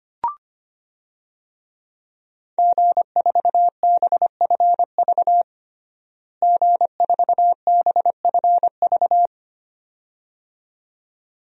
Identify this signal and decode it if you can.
Morse code, heard as a keyed tone, about 25 wpm, 700 Hz. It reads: G4BFV G4BFV
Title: DXCC-100 Call Signs - 5 Characters - Repeated 2x - 25wpm